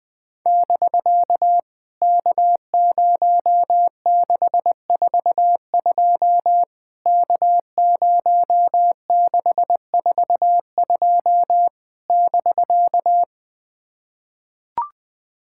Morse code, heard as a keyed tone, about 20 wpm, 700 Hz. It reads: <BK> K0642 K0642 <BK>
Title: POTA QSO Practice - 20wpm